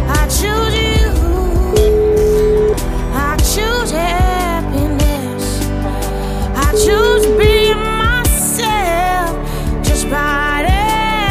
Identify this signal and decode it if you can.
Ringback tone (Europe, 425 Hz), with noise.